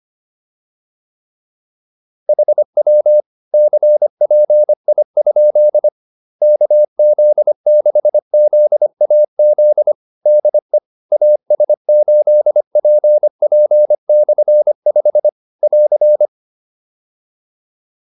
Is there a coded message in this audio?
Yes: Morse code from 2.3 to 16.3 s, heard as a keyed tone, about 25 wpm, 600 Hz. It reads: HW CPI? KZ6ZAZ DE AS8PP/5 <AR>